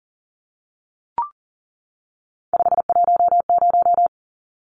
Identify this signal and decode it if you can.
Morse code, heard as a keyed tone, about 40 wpm, 700 Hz. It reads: 510
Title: Numbers - 3-Digit 40wpm